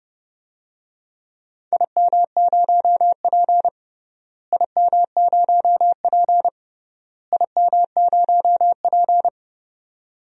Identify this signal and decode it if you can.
Morse code, heard as a keyed tone, about 32 wpm, 700 Hz. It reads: IM0P IM0P IM0P
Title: DXCC-100 Call Signs - 4 Characters - Repeated 3x - 30wpm